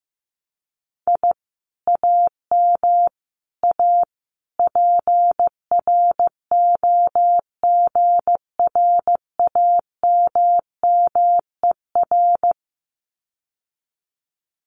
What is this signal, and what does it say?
Morse code, heard as a keyed tone, about 15 wpm, 700 Hz. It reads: I AM A PROGRAMMER
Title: Common QSO Phrases Encore 15wpm